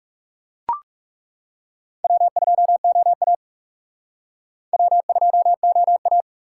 Morse code, heard as a keyed tone, about 37 wpm, 700 Hz. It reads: W2OA W2OA
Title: Speed Racing - USA 1x2 Call Signs 30wpm